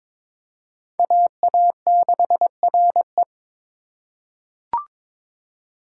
Morse code, heard as a keyed tone, about 22 wpm, 700 Hz. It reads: AA6RE
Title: K1USN SST QSO Practice - 22wpm